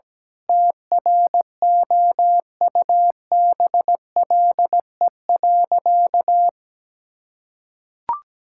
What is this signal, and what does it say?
Morse code, heard as a keyed tone, about 17 wpm, 700 Hz. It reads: TROUBLE.
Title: Sentences from Top 1600 Words 17wpm